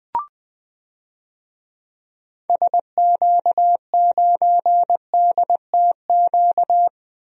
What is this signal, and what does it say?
Morse code, heard as a keyed tone, about 20 wpm, 700 Hz. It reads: SQ9DTQ
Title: DXCC-100 Call Signs - 6 Characters - 20wpm